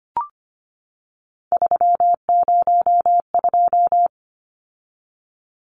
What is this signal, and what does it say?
Morse code, heard as a keyed tone, about 25 wpm, 700 Hz. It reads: 302
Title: Numbers - 3-Digit 25wpm